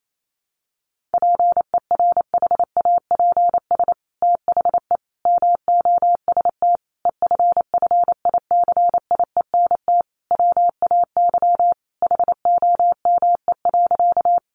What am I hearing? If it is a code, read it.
Morse code, heard as a keyed tone, about 28 wpm, 700 Hz. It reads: PERHAPS THE MOST EFFICIENT WAY HOME.
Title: Sentences from Top 1300 Words 28wpm